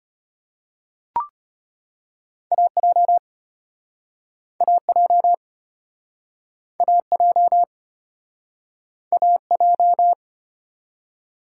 Morse code, heard as a keyed tone, about 33 wpm, 700 Hz. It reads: AJ AJ AJ AJ